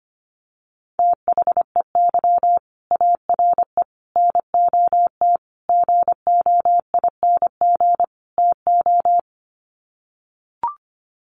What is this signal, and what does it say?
Morse code, heard as a keyed tone, about 25 wpm, 700 Hz. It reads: THEY ARE NOT GOING TO
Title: Sets of 5 Words Encore2 25wpm